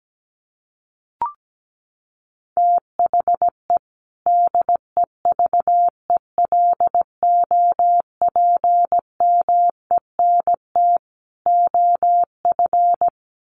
Morse code, heard as a keyed tone, about 17 wpm, 700 Hz. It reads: THE DEVELOPMENT OF
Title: Sets of 3 Words 17wpm